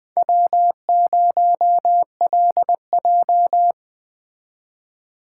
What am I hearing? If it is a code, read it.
Morse code, heard as a keyed tone, about 20 wpm, 700 Hz. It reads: W0LJ